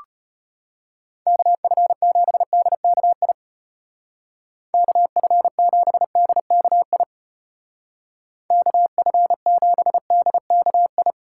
Morse code, heard as a keyed tone, about 34 wpm, 700 Hz. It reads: KF7DKI KF7DKI KF7DKI